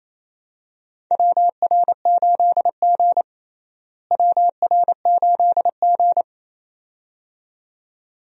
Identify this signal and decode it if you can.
Morse code, heard as a keyed tone, about 28 wpm, 700 Hz. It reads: WR8G WR8G
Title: USA Call Signs - 4 Characters - Repeated 2x - 28wpm